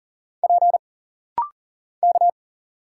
Morse code, heard as a keyed tone, about 39 wpm, 700 Hz. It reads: P K
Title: Single Letters - Rapid-Fire 40wpm